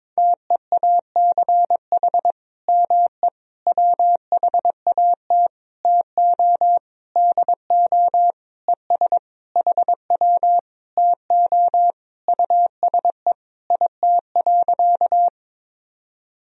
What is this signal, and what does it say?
Morse code, heard as a keyed tone, about 22 wpm, 700 Hz. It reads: TEACH ME WHAT TO DO ES HW TO USE IT.
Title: Sentences from Top 500 Words and Abbreviations 22wpm